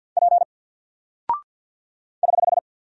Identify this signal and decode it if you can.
Morse code, heard as a keyed tone, about 48 wpm, 700 Hz. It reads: P <HH>
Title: Single Character (Letter:Number:Punctuation:Prosign) 50wpm